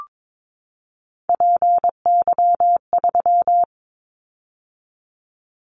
Morse code, heard as a keyed tone, about 22 wpm, 700 Hz. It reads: PY3
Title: Callsign Prefixes for Top 100 DXCC - 22wpm